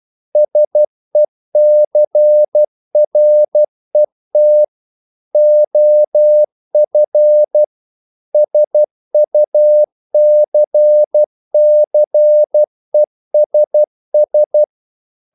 Morse code, heard as a keyed tone, about 12 wpm, 600 Hz. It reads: SECRET OF SUCCESS
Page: English quotations in CW (telegraphy - morse)